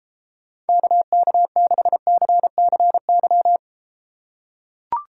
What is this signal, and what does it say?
Morse code, heard as a keyed tone, about 32 wpm, 700 Hz. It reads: KK6CCY